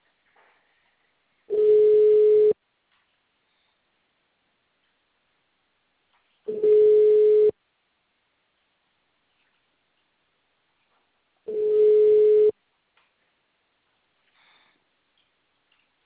Ringback tone (Europe, 425 Hz).